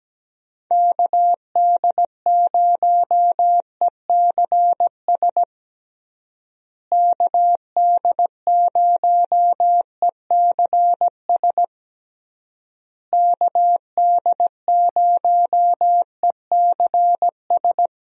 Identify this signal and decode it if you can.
Morse code, heard as a keyed tone, about 17 wpm, 700 Hz. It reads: KD0ECS KD0ECS KD0ECS
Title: USA Call Signs - 6 Characters - Repeated 3x - 17wpm